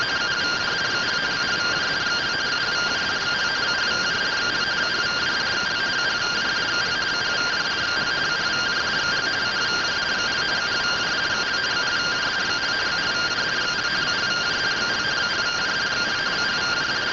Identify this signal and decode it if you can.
Two-tone teleprinter signal, about 170 Hz shift.